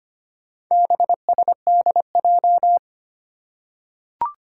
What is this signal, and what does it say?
Morse code, heard as a keyed tone, about 25 wpm, 700 Hz. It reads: BSDJ
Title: Four Challenging Letter-Number Combinations 25wpm